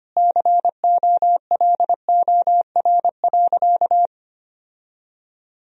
Morse code, heard as a keyed tone, about 25 wpm, 700 Hz. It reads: COLOR.
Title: Sentences from Top 1000 Words - Review Part 4 - 25wpm